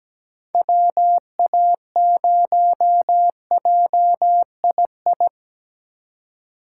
Morse code, heard as a keyed tone, about 20 wpm, 700 Hz. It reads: WA0JII
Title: Speed Racing - USA Call Signs 17wpm